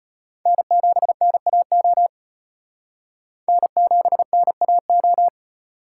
Morse code, heard as a keyed tone, about 34 wpm, 700 Hz. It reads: N7NAO N7NAO